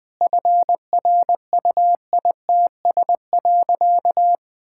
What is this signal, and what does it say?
Morse code, heard as a keyed tone, about 20 wpm, 700 Hz. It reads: FRUITS.
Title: Sentences with 7-Word Phrases - 20wpm